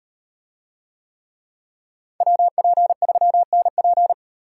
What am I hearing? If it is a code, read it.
Morse code, heard as a keyed tone, about 36 wpm, 700 Hz. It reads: WP3NP